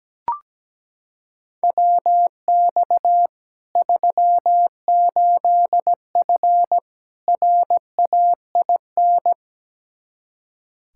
Morse code, heard as a keyed tone, about 17 wpm, 700 Hz. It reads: WX 38F RAIN
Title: Practice QSO - Exchange 2 - Info - 17wpm